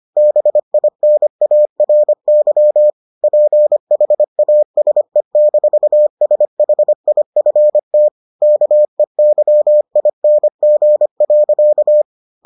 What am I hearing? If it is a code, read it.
Morse code, heard as a keyed tone, about 25 wpm, 600 Hz. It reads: BINARY PHASE-SHIFT KEYING.